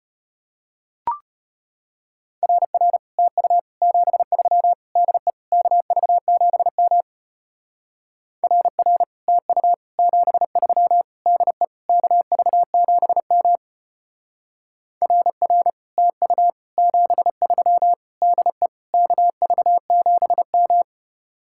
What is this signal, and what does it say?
Morse code, heard as a keyed tone, about 32 wpm, 700 Hz. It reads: RR TU 73 DE KV7M RR TU 73 DE KV7M RR TU 73 DE KV7M